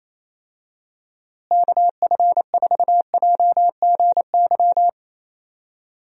Morse code, heard as a keyed tone, about 28 wpm, 700 Hz. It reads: KF4JGY
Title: USA Call Signs - 6 Characters - Repeated 2x - 28wpm